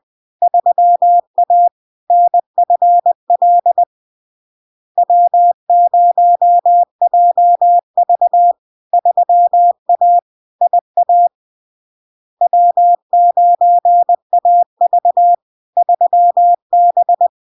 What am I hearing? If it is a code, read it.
Morse code, heard as a keyed tone, about 20 wpm, 700 Hz. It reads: 3A NFL W0JV 3A IA W9AV 3B
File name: LARK-FD-2022-Log-in-CW-20wpm-700-Hz